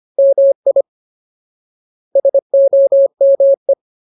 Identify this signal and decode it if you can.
Morse code, heard as a keyed tone, about 25 wpm, 550 Hz. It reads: MI SOME
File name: Ultimate Word List / 25WPM Ultimate